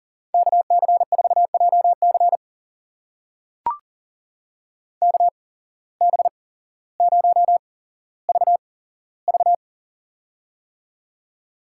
Morse code, heard as a keyed tone, about 40 wpm, spaced as for 17 wpm, 700 Hz. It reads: KC4JC KB0VV
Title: USA Call Signs - 5 Characters - 8x Character Spacing - 40wpm